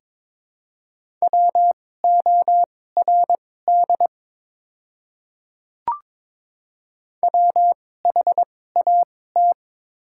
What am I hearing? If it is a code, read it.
Morse code, heard as a keyed tone, about 22 wpm, spaced as for 16 wpm, 700 Hz. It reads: WORD WHAT
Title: 2x Character Spacing - Top 300 Words - 22wpm